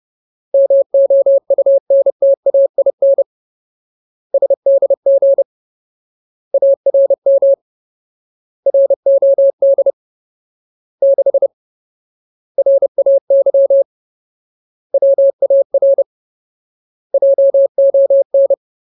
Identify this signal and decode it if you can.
Morse code, heard as a keyed tone, about 30 wpm, 550 Hz. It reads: MOUNTAIN SDG ARM ROD 6 RAY WAR JON
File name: Ultimate Word List / 30WPM Ultimate